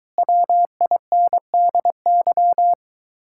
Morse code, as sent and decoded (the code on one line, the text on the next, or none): .-- .. -. -.. -.--
WINDY